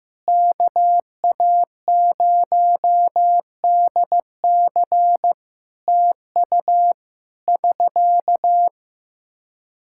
Morse code, heard as a keyed tone, about 15 wpm, 700 Hz. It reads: KA0DC TU <SK>
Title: Practice QSO - Exchange 4 - Ending - 15wpm